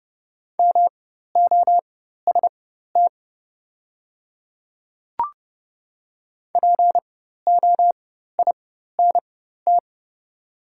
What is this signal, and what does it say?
Morse code, heard as a keyed tone, about 31 wpm, spaced as for 14 wpm, 700 Hz. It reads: MOST POINT
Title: 4x Character Spacing - Top 200 Words - 30wpm